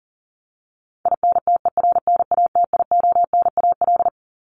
Morse code, heard as a keyed tone, about 38 wpm, 700 Hz. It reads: INTERNATIONAL
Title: Top 501-750 Words 40wpm